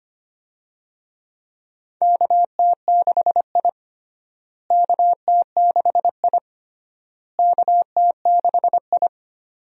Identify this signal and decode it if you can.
Morse code, heard as a keyed tone, about 25 wpm, 700 Hz. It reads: KT6I KT6I KT6I
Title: USA Call Signs - 4 Characters - Repeated 3x - 25wpm